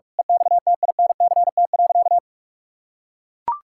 Morse code, heard as a keyed tone, about 41 wpm, 700 Hz. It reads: EXTINCT.
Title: Pangram Challenge v2 - 45wpm